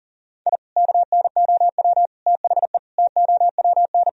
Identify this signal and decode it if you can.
Morse code, heard as a keyed tone, about 37 wpm, 700 Hz. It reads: I KNOW THE TOWN
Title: Sentences from Top 1000 Words - Review Part 2 - 40wpm